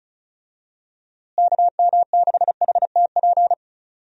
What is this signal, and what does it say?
Morse code, heard as a keyed tone, about 36 wpm, 700 Hz. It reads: KM6HTP
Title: USA Call Signs - 6 Characters - Repeated 2x - 35wpm